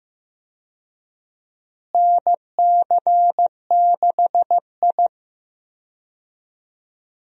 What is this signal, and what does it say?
Morse code, heard as a keyed tone, about 15 wpm, 700 Hz. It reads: NC6I
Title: USA Call Signs - 4 Characters - Repeated 2x - 15wpm